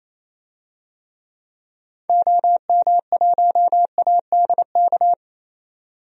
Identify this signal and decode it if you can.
Morse code, heard as a keyed tone, about 28 wpm, 700 Hz. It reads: OM1ADK
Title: DXCC-100 Call Signs - 6 Characters - 28wpm